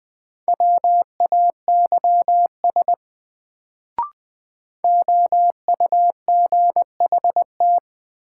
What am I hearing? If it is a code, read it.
Morse code, heard as a keyed tone, about 20 wpm, 700 Hz. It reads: WAYS OUGHT
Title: Sets of 5 Words Encore - New Word Drill - 20wpm